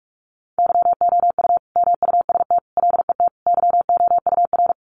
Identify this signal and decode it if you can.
Morse code, heard as a keyed tone, about 44 wpm, 700 Hz. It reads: YOU MUST LET YOUR
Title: Sentences from Top 1500 Words 45wpm